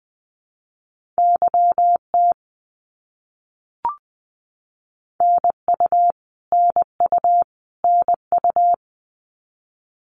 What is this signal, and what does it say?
Morse code, heard as a keyed tone, about 20 wpm, 700 Hz. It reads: YT NU NU NU
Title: Canadian Provinces and Territories - Repeated 3x - 20wpm